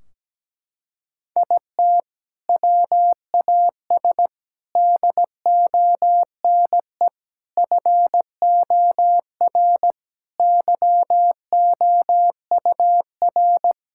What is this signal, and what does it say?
Morse code, heard as a keyed tone, about 17 wpm, 700 Hz. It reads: IT WAS DONE FOR YOUR
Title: Sentences from Top 1600 Words 17wpm